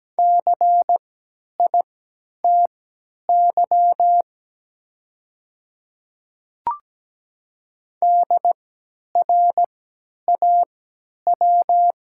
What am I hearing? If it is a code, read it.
Morse code, heard as a keyed tone, about 17 wpm, spaced as for 10 wpm, 700 Hz. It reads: CITY DRAW
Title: 3x Character Spacing - Top 500 Words - 17wpm